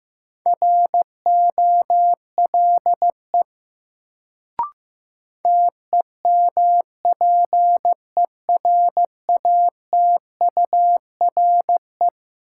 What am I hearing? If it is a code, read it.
Morse code, heard as a keyed tone, about 15 wpm, 700 Hz. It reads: ROLE TEMPERATURE
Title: Sentences from Top 1500 Words - New Word Drill Alt - 15wpm